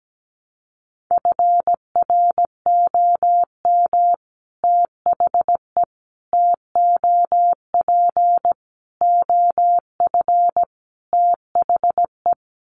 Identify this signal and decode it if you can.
Morse code, heard as a keyed tone, about 17 wpm, 700 Hz. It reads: FROM THE TOP OF THE
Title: Sets of 5 Words Encore 17wpm